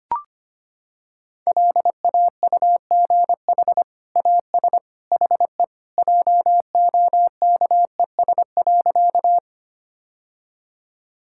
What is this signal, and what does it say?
Morse code, heard as a keyed tone, about 25 wpm, 700 Hz. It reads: LAUGH AS HE JOKES.
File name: practice / 25wpm / Sentences with 2-Word Phrases - 25wpm